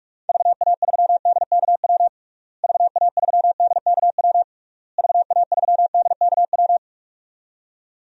Morse code, heard as a keyed tone, about 46 wpm, 700 Hz. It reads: VA3DKW VA3DKW VA3DKW
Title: DXCC-100 Call Signs - 6 Characters - Repeated 3x - 45wpm